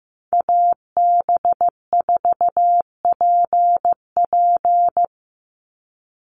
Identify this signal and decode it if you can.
Morse code, heard as a keyed tone, about 15 wpm, 700 Hz. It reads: AB4PP
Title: NA SOTA Chaser Callsigns 15wpm